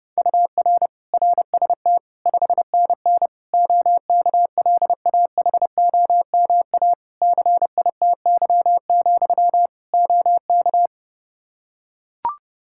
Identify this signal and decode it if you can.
Morse code, heard as a keyed tone, about 30 wpm, 700 Hz. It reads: UR RST 5NN OKLAHOMA CITY, OK
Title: Practice QSO - Exchange 1 - Beginning - 30wpm